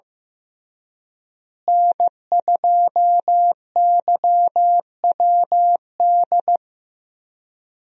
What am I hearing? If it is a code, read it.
Morse code, heard as a keyed tone, about 15 wpm, 700 Hz. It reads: N2YWD
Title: USA Call Signs - 5 Characters - Repeated 3x - 15wpm